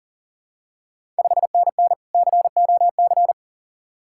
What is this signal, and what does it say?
Morse code, heard as a keyed tone, about 40 wpm, 690 Hz. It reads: 5NN COC
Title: Ontario QSO Party Exchange 40wpm